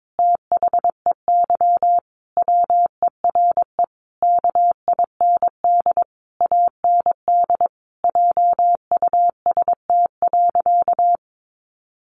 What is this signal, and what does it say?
Morse code, heard as a keyed tone, about 22 wpm, 700 Hz. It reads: THEY WERE KIND AND JUST.
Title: Sentences from Top 800 Words 22wpm